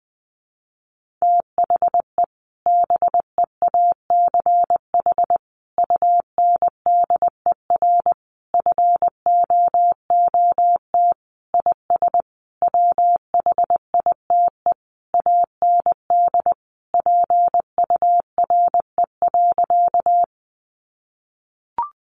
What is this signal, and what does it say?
Morse code, heard as a keyed tone, about 20 wpm, 700 Hz. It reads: THE BEACH UNDER FOOT IS WHITE AND PURE.
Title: Sentences from Top 1500 Words 20wpm